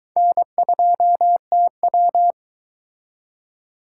Morse code, heard as a keyed tone, about 22 wpm, 700 Hz. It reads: N2TW